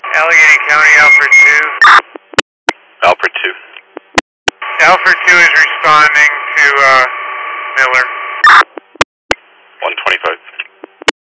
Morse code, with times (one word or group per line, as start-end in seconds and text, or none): none